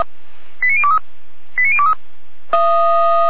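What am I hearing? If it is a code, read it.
Five-tone selective call (ZVEI), heard as quick runs of five tones, tones of about 70 ms. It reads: 89023 89023
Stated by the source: ZVEI five-tone selective call received by SDR.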